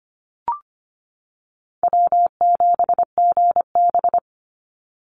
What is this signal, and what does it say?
Morse code, heard as a keyed tone, about 25 wpm, 700 Hz. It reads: W7GB
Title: Easy Call Signs - 25wpm